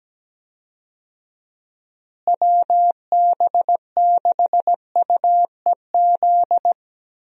Morse code, heard as a keyed tone, about 17 wpm, 700 Hz. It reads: WB6UEZ
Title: USA Call Signs - 6 Characters - Repeated 2x - 17wpm